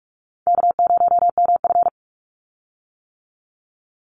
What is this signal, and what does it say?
Morse code, heard as a keyed tone, about 44 wpm, 700 Hz. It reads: K0MF
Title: USA Call Signs 45wpm